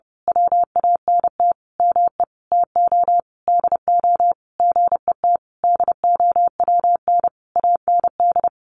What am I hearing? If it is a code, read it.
Morse code, heard as a keyed tone, about 30 wpm, 700 Hz. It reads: WANT ME TO DO GET DOWN AND
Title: A Sound Of Thunder 30wpm